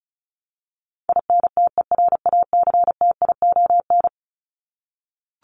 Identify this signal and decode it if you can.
Morse code, heard as a keyed tone, about 34 wpm, 700 Hz. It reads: INTERACTION
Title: Top 1251-1500 Words 35wpm